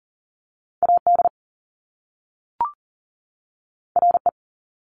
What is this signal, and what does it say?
Morse code, heard as a keyed tone, about 39 wpm, 700 Hz. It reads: AD RE
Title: Top Two Letter Combinations in Words 40wpm